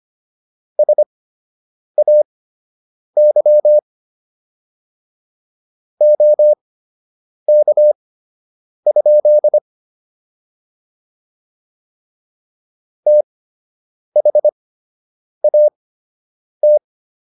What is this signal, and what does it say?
Morse code, heard as a keyed tone, about 25 wpm, spaced as for 8 wpm, 600 Hz. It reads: SAY OK? THAT